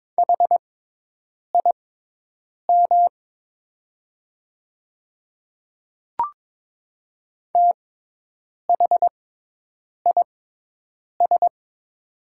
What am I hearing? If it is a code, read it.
Morse code, heard as a keyed tone, about 22 wpm, spaced as for 8 wpm, 700 Hz. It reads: HIM THIS
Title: 6x Character Spacing - Top 100 Words - 22wpm